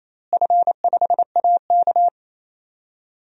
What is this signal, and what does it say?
Morse code, heard as a keyed tone, about 28 wpm, 700 Hz. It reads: F5AK
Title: DXCC-100 Call Signs - 4 Characters - Repeated 2x - 28wpm